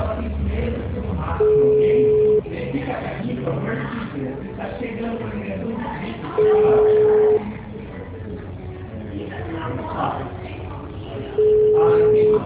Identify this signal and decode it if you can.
Ringback tone (Europe, 425 Hz), with speech.
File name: out- -2008- - - .4610